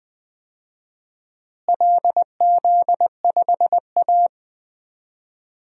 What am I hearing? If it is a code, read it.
Morse code, heard as a keyed tone, about 21 wpm, 700 Hz. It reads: LZ5A
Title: DXCC-100 Call Signs - 4 Characters - Repeated 3x - 20wpm